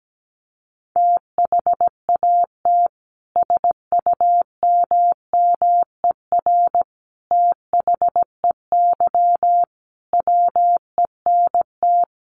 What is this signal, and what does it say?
Morse code, heard as a keyed tone, about 17 wpm, 700 Hz. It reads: THAT SUMMER THEY WENT
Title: Sentences from Top 1100 Words 17wpm